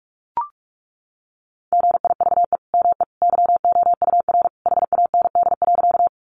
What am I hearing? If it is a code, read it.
Morse code, heard as a keyed tone, about 44 wpm, 700 Hz. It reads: GIVE ME YOUR HAND.
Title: Sentences from Top 200 Words 45wpm